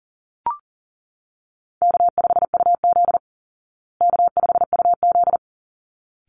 Morse code, heard as a keyed tone, about 40 wpm, 700 Hz. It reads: K5UZ K5UZ
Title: Easy Call Signs Repeated 2x - 40wpm